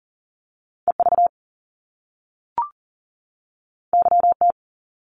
Morse code, heard as a keyed tone, about 41 wpm, 700 Hz. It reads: EV YT
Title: Top Two Letter Combinations in Words 40wpm